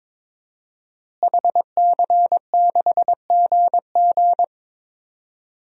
Morse code, heard as a keyed tone, about 22 wpm, 700 Hz. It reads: HC6GG
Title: DXCC-100 Call Signs - 5 Characters - Repeated 2x - 22wpm